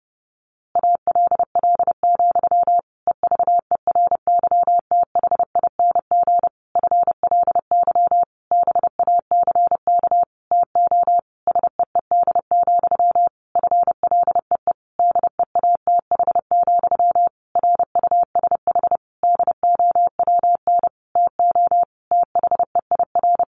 Morse code, heard as a keyed tone, about 30 wpm, 700 Hz. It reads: ALL, EVERYTHING FLY BACK TO SEED, FLEE DEATH, RUSH DOWN TO THEIR
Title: A Sound Of Thunder 30wpm